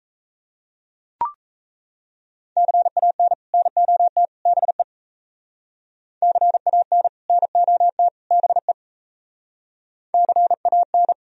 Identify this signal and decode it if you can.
Morse code, heard as a keyed tone, about 33 wpm, 700 Hz. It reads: CAN NOT BE CAN NOT BE CAN
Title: Speed-Racing - Sets of 3 Words 28wpm